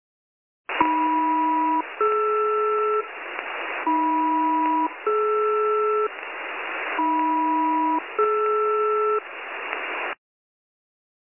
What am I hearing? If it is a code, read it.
Aircraft SELCAL, heard as two pairs of tones. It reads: AM-DR AM-DR AM-DR